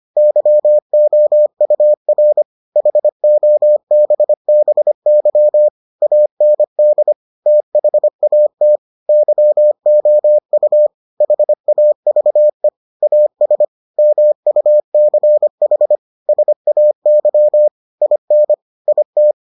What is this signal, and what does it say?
Morse code, heard as a keyed tone, about 25 wpm, 600 Hz. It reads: YOUR HOBBY AND THAT YOU HAVE AS MUCH SAY IN IT